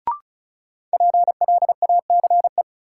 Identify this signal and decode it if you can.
Morse code, heard as a keyed tone, about 36 wpm, 700 Hz. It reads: PLACE
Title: Sets of 4 Words - New Word Drill - 35wpm